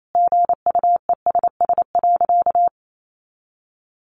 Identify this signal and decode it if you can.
Morse code, heard as a keyed tone, about 28 wpm, 700 Hz. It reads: GUESS.
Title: Sentences from Top 1500 Words 28wpm